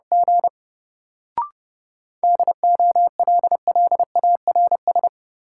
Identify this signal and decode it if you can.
Morse code, heard as a keyed tone, about 31 wpm, 700 Hz. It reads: G DOLLARS
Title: Sets of 5 Words - New Word Drill - 30wpm